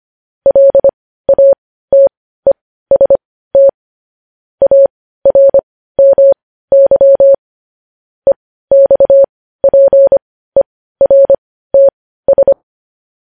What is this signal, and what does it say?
Morse code, heard as a keyed tone, about 25 wpm, spaced as for 15 wpm, 550 Hz. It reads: LATEST ARMY EXPERTS
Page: news-headlines-in-morse-code-at-15-wpm